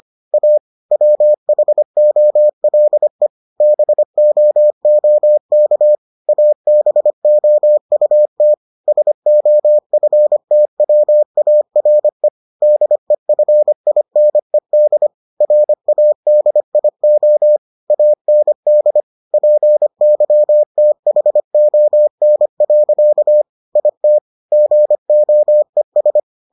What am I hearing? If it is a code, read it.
Morse code, heard as a keyed tone, about 25 wpm, 600 Hz. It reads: A WHOLE BOOK ABOUT SOFTWARE DEFINED RADIO AND PYTHON. IT GOES